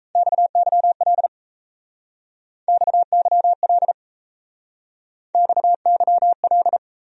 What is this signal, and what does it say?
Morse code, heard as a keyed tone, about 36 wpm, 700 Hz. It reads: XYL XYL XYL